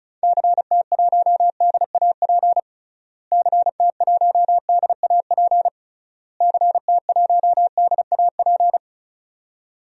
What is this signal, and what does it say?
Morse code, heard as a keyed tone, about 34 wpm, 700 Hz. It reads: CT1DAP CT1DAP CT1DAP
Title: DXCC-100 Call Signs - 6 Characters - Repeated 3x - 35wpm